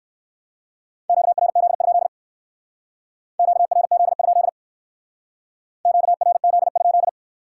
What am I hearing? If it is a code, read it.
Morse code, heard as a keyed tone, about 58 wpm, 700 Hz. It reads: QRZ? QRZ? QRZ?